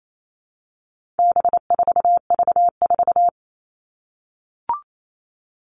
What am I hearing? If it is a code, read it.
Morse code, heard as a keyed tone, about 28 wpm, 700 Hz. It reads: B4V4
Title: All Cut Numbers - 4-Digit 28wpm